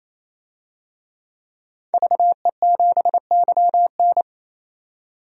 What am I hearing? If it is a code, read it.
Morse code, heard as a keyed tone, about 28 wpm, 700 Hz. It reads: VE7YN
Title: DXCC-100 Call Signs - 5 Characters - 28wpm